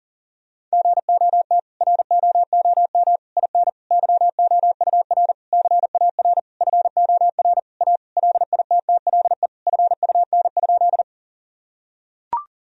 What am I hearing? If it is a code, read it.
Morse code, heard as a keyed tone, about 40 wpm, 700 Hz. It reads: GOT ROOM IN YOUR CAR FOR A LITTLE FUN?
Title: Sentences from Top 1000 Words - Review Part 1 - 40wpm